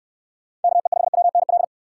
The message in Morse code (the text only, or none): KE5WNL